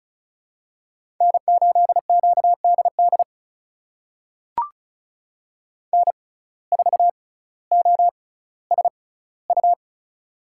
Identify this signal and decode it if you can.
Morse code, heard as a keyed tone, about 34 wpm, spaced as for 18 wpm, 700 Hz. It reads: N8QDD N4OSU
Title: USA Call Signs - 5 Characters - 6x Character Spacing - 35wpm